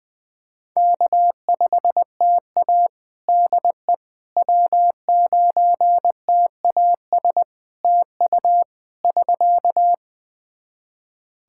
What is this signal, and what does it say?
Morse code, heard as a keyed tone, about 20 wpm, 700 Hz. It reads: K5TA DE W9TAS TU <SK>
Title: Practice QSO - Exchange 4 - Ending - 20wpm